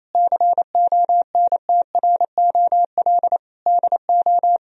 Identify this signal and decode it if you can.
Morse code, heard as a keyed tone, about 28 wpm, 700 Hz. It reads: CONTROL DO
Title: Sentences from Top 1000 Words - Review Part 3 - 28wpm